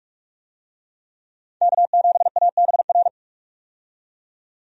Morse code, heard as a keyed tone, about 46 wpm, 700 Hz. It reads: K7ABR
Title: USA Call Signs - 5 Characters - 45wpm